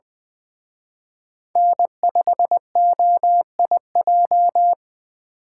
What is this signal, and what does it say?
Morse code, heard as a keyed tone, about 24 wpm, 700 Hz. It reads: N5OIJ
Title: Speed Racing - USA 1x3 Call Signs 20wpm